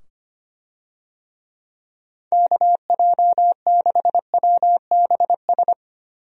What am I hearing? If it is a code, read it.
Morse code, heard as a keyed tone, about 25 wpm, 700 Hz. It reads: KJ6WBS
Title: USA Call Signs - 6 Characters - 25wpm